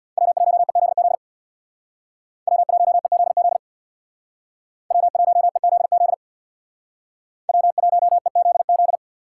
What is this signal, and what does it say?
Morse code, heard as a keyed tone, about 57 wpm, 700 Hz. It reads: W1EZZ W1EZZ W1EZZ W1EZZ
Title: Speed Racing - USA 1x3 Call Signs 50wpm